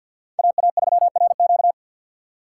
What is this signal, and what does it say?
Morse code, heard as a keyed tone, about 47 wpm, 700 Hz. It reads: AA3RQ
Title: USA Call Signs - 5 Characters - 3x Character Spacing - 50wpm